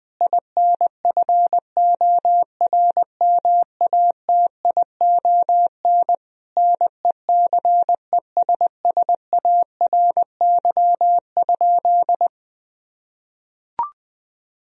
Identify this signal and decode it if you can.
Morse code, heard as a keyed tone, about 20 wpm, 700 Hz. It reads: INFORMATION NECESSARY?
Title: Sentences from Top 1000 Words - Review Part 3 - 20wpm